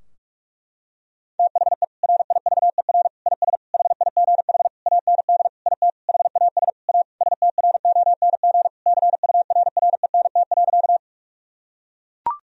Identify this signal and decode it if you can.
Morse code, heard as a keyed tone, about 41 wpm, 700 Hz. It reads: THE RIVER IS HIGH AND IT HAS A STRONG CURRENT.
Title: Sentences from Top 1000 Words - Review Part 4 - 45wpm